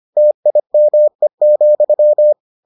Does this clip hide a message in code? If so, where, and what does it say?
Yes: Morse code from 0.2 to 2.3 s, heard as a keyed tone, about 25 wpm, 600 Hz. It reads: TIME,